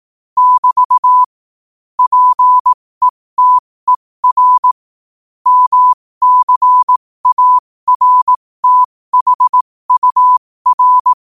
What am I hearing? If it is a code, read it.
Morse code, heard as a keyed tone, about 18 wpm, spaced as for 15 wpm, 1000 Hz. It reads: <BT> PETER MCARTHUR
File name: qotdfiles / 15wpm